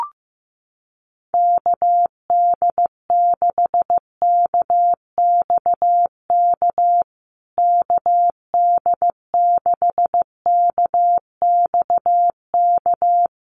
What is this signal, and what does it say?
Morse code, heard as a keyed tone, about 15 wpm, 700 Hz. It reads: KD6KXK KD6KXK
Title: USA Call Signs Repeated 2x 15wpm